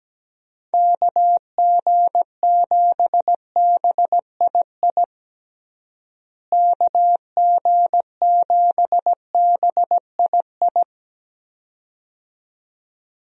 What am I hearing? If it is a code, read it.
Morse code, heard as a keyed tone, about 17 wpm, 700 Hz. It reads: KG7BII KG7BII
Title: USA Call Signs - 6 Characters - Repeated 2x - 17wpm